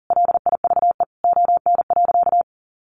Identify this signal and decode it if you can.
Morse code, heard as a keyed tone, about 40 wpm, 700 Hz. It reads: LIVE ON.
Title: Sentences from Top 800 Words 40wpm